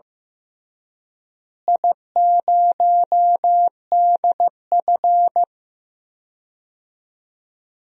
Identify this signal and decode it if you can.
Morse code, heard as a keyed tone, about 15 wpm, 700 Hz. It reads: I0DF
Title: DXCC-100 Call Signs - 4 Characters - Repeated 3x - 15wpm